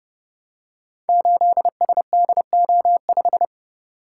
Morse code, heard as a keyed tone, about 28 wpm, 700 Hz. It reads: 8SDO5
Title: Code Groups - 5 Letter:Numbers - 30wpm